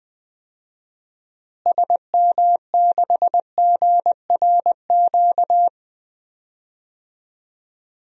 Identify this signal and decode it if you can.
Morse code, heard as a keyed tone, about 20 wpm, 700 Hz. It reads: SM6GRQ
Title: DXCC-100 Call Signs - 6 Characters - Repeated 2x - 20wpm